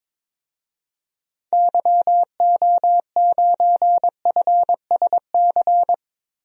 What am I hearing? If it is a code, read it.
Morse code, heard as a keyed tone, about 22 wpm, 700 Hz. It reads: YO9FSC
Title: DXCC-100 Call Signs - 6 Characters - Repeated 2x - 22wpm